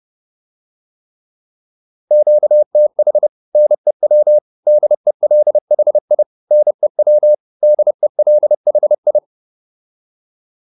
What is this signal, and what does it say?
Morse code, heard as a keyed tone, about 30 wpm, 600 Hz. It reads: QTH NEW DELHI NEW DELHI